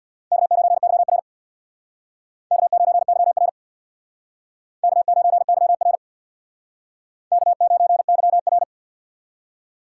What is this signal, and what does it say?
Morse code, heard as a keyed tone, about 63 wpm, 700 Hz. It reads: K9YR K9YR K9YR K9YR